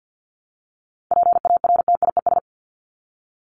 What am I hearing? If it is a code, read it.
Morse code, heard as a keyed tone, about 50 wpm, 700 Hz. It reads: PARTIES
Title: ES:S Suffix - Top 100 Words with Base Words 50wpm